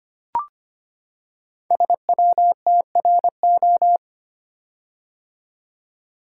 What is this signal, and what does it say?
Morse code, heard as a keyed tone, about 25 wpm, 700 Hz. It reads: SWTRO
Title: Code Groups - 5 Letters - 25wpm